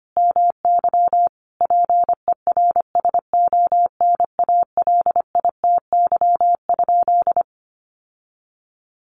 Morse code, heard as a keyed tone, about 25 wpm, 700 Hz. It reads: MY PERSONALITY?
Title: Sentences from Top 1400 Words 25wpm